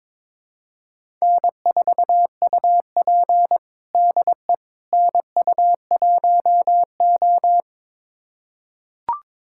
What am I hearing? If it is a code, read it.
Morse code, heard as a keyed tone, about 22 wpm, 700 Hz. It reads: N4UP DE NU1O
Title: USA Callsign DE Callsign - 4 Characters - 22wpm